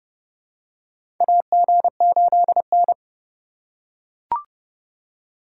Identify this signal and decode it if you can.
Morse code, heard as a keyed tone, about 30 wpm, 700 Hz. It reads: AG8N
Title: USA Call Signs - 4 Characters - 6x Character Spacing - 30wpm